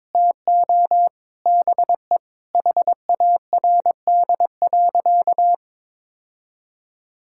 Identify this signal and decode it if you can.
Morse code, heard as a keyed tone, about 22 wpm, 700 Hz. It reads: TO BE HARD.
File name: practice / 22wpm / Sentences with 4-Word Phrases - 22wpm